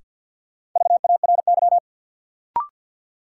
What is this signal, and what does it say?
Morse code, heard as a keyed tone, about 53 wpm, 700 Hz. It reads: VARY
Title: Sentences from Top 1200 Words - New Word Drill - 50wpm